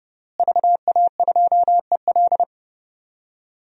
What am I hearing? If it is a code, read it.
Morse code, heard as a keyed tone, about 30 wpm, 700 Hz. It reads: VA2EL
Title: DXCC-100 Call Signs - 5 Characters - Repeated 2x - 30wpm